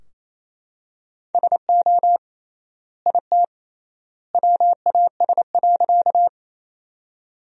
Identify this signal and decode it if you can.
Morse code, heard as a keyed tone, about 28 wpm, 700 Hz. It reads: SO IT WAS.
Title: 3x Word-Spacing - Sentences from Top 100 Words - 28wpm